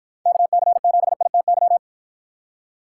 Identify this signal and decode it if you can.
Morse code, heard as a keyed tone, about 50 wpm, 700 Hz. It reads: KC7ITY